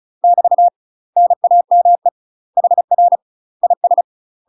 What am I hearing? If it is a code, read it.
Morse code, heard as a keyed tone, about 36 wpm, 700 Hz. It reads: <BT> NAME HR IS
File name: UK MASTER QSOs TRAINING / 35WPM / UK MASTER QSOs 0009